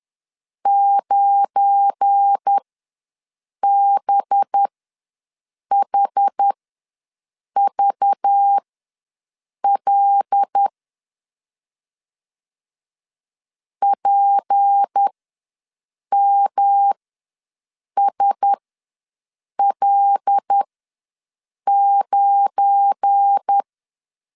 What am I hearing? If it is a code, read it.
Morse code, heard as a keyed tone, about 10 wpm, spaced as for 6 wpm, 800 Hz. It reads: 9BHVL PMSL9
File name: LEZIONI / 7 GF9Z / 7 GF9 ws 10wpm 4034